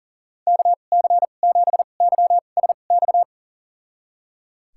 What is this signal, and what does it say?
Morse code, heard as a keyed tone, about 37 wpm, spaced as for 28 wpm, 700 Hz. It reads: KC7YSX
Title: USA Call Signs - 2x Character Spacing - 40wpm